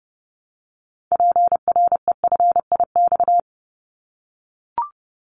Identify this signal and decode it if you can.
Morse code, heard as a keyed tone, about 29 wpm, 700 Hz. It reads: PREFIX
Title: Callsign Prefixes for Top 25 DXCC - Repeated 3x - 30wpm